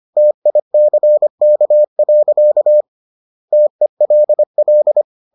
Morse code, heard as a keyed tone, about 25 wpm, 600 Hz. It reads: TICK. TELL